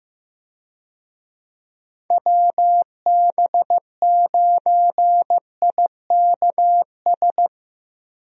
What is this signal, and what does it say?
Morse code, heard as a keyed tone, about 15 wpm, 700 Hz. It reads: WB9IKS
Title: USA Call Signs - 6 Characters - Repeated 2x - 15wpm